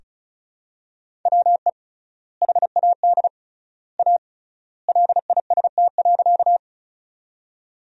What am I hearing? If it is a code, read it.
Morse code, heard as a keyed tone, about 34 wpm, 700 Hz. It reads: WE HAD A LIST.
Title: 3x Word-Spacing - Sentences from Top 400 Words - 35wpm